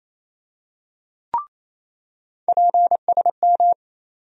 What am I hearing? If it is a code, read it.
Morse code, heard as a keyed tone, about 28 wpm, 700 Hz. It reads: PSM
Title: USA Call Sign Postfixes - 3 Letters 28wpm